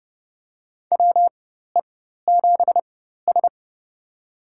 Morse code, heard as a keyed tone, about 30 wpm, spaced as for 14 wpm, 700 Hz. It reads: WE7S
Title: USA Call Signs - 4 Characters - 4x Character Spacing - 30wpm